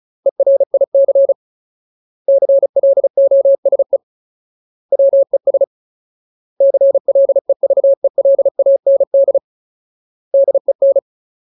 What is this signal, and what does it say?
Morse code, heard as a keyed tone, about 34 wpm, 550 Hz. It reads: ERIC CLOSE WES CLEVELAND DEN
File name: Ultimate Word List / 35WPM Ultimate